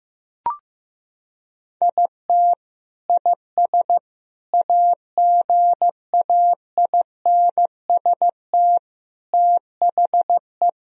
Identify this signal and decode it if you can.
Morse code, heard as a keyed tone, about 15 wpm, 700 Hz. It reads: IT IS AGAINST THE
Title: Sentences from Top 1000 Words - Review Part 3 - 15wpm